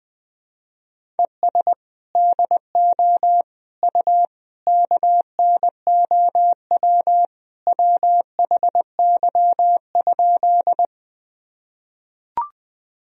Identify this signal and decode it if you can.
Morse code, heard as a keyed tone, about 21 wpm, 700 Hz. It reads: ES DO U KNOW WHY?
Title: Sentences from Top 200 Words and Abbreviations 20wpm